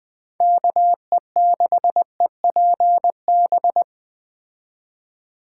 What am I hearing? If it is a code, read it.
Morse code, heard as a keyed tone, about 20 wpm, 700 Hz. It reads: KE6EPB